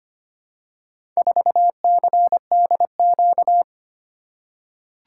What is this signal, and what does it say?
Morse code, heard as a keyed tone, about 25 wpm, 700 Hz. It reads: 4CDQ
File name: practice / 25wpm / Four Letter-Number Combinations - 25wpm